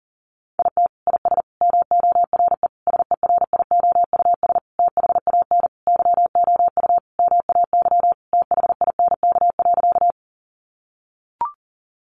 Morse code, heard as a keyed tone, about 40 wpm, 700 Hz. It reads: IT IS MORE SERIOUS THAN YOU MAY THINK.
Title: Sentences from Top 1100 Words 40wpm